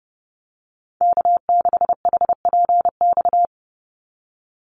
Morse code, heard as a keyed tone, about 30 wpm, 700 Hz. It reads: K6HPX
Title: NA SOTA Activator Callsigns 30wpm